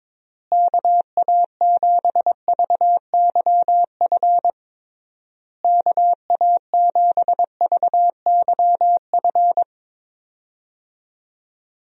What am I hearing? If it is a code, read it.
Morse code, heard as a keyed tone, about 22 wpm, 700 Hz. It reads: KA7VYF KA7VYF
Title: USA Call Signs - 6 Characters - Repeated 3x - 22wpm